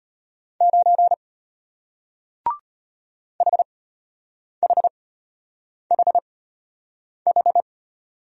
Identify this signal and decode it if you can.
Morse code, heard as a keyed tone, about 34 wpm, spaced as for 8 wpm, 700 Hz. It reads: 9 HHHH